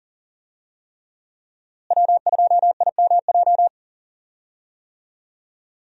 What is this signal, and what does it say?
Morse code, heard as a keyed tone, about 40 wpm, 700 Hz. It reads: W2IMJ
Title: USA Call Signs - 5 Characters - Repeated 3x - 40wpm